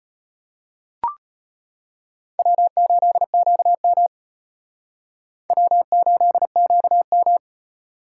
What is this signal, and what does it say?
Morse code, heard as a keyed tone, about 33 wpm, 700 Hz. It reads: W8QM W8QM